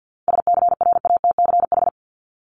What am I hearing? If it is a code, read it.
Morse code, heard as a keyed tone, about 50 wpm, 700 Hz. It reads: SCRATCH
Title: Top 1501-1750 Words 50wpm